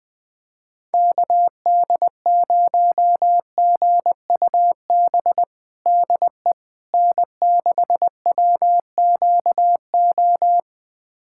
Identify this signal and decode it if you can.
Morse code, heard as a keyed tone, about 21 wpm, 700 Hz. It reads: KD0GUB DE N6WQO
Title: USA Callsign DE Callsign 20wpm